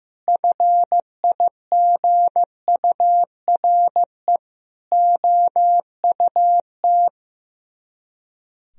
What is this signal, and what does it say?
Morse code, heard as a keyed tone, about 15 wpm, 700 Hz. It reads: FIGURE OUT
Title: Speed-Racing - Sets of 4 Words 15wpm